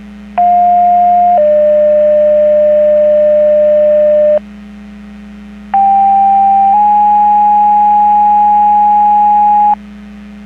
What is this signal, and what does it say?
Two-tone page, heard as a one-second tone, then a three-second tone, about 710 then 600 Hz, about 790 then 830 Hz. It is heard from 0.4 to 9.7 s.
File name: Portage 154.385 Paging